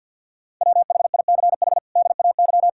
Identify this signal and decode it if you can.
Morse code, heard as a keyed tone, about 50 wpm, 700 Hz. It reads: WHICH DAY
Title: Sentences from Top 1000 Words - Review Part 5 - 50wpm